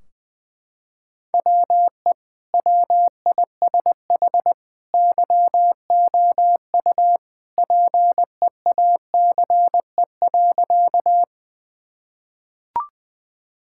Morse code, heard as a keyed tone, about 21 wpm, 700 Hz. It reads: WE WISH YOU PEACE.
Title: Sentences from Top 1600 Words 20wpm